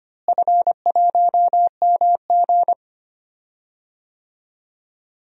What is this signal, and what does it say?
Morse code, heard as a keyed tone, about 25 wpm, 700 Hz. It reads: F1MG
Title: DXCC-100 Call Signs - 4 Characters - Repeated 3x - 25wpm